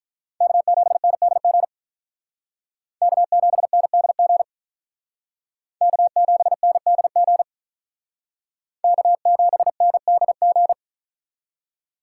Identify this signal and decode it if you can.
Morse code, heard as a keyed tone, about 45 wpm, 700 Hz. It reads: K7NDG K7NDG K7NDG K7NDG